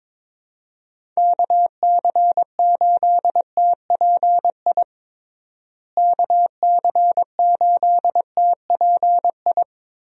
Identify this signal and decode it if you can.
Morse code, heard as a keyed tone, about 22 wpm, 700 Hz. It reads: KC8TPI KC8TPI
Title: USA Call Signs - 6 Characters - Repeated 2x - 22wpm